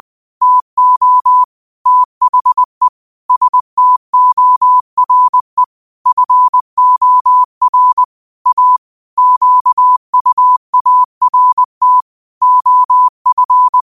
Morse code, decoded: TO THE STORE FOR A QUART OF

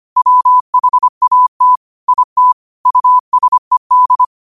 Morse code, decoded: WHAT IT USED

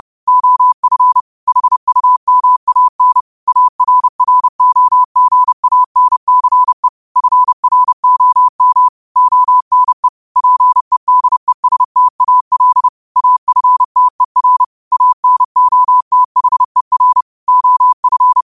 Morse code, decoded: OF HUMAN ARROGANCE FROM ONE PEDESTAL AFTER ANOTHER OF

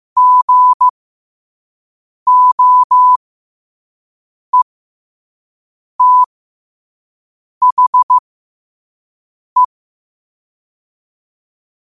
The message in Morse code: GOETHE

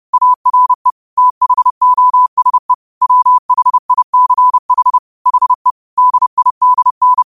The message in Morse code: ARE THOSE WHICH HE DIDN

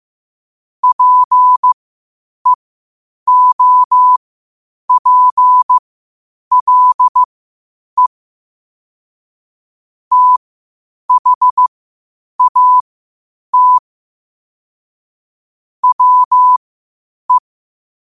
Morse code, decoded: PEOPLE THAT WE

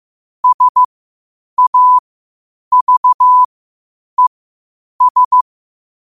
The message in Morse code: SAVES

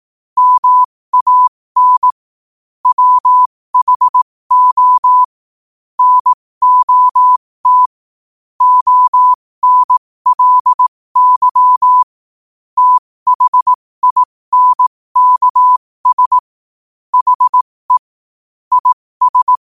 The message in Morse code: MAN WHO NOT ONLY THINKS HE IS

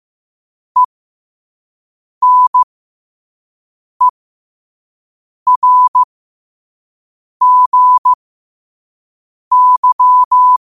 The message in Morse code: ENERGY

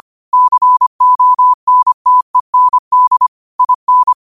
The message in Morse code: CONTEND IN

